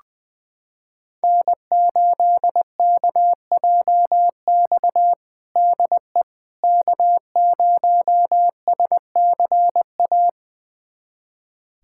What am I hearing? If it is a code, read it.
Morse code, heard as a keyed tone, about 21 wpm, 700 Hz. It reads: N8KJX DE K0SCA
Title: USA Callsign DE Callsign - 5 Characters - 20wpm